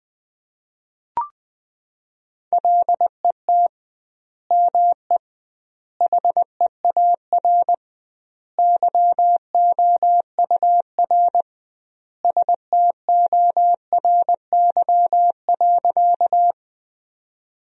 Morse code, heard as a keyed tone, about 20 wpm, 700 Hz. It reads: LET ME HEAR YOUR STORY.
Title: 2x Word-Spacing - Sentences from Top 300 Words - 20wpm